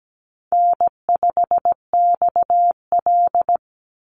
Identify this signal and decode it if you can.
Morse code, heard as a keyed tone, about 17 wpm, 700 Hz. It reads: N5XL
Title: NA SOTA Chaser Callsigns 17wpm